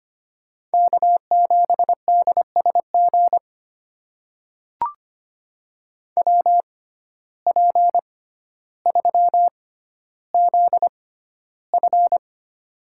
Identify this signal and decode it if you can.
Morse code, heard as a keyed tone, about 25 wpm, spaced as for 13 wpm, 700 Hz. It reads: K7DSG WP3ZF
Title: USA Call Signs - 5 Characters - 6x Character Spacing - 25wpm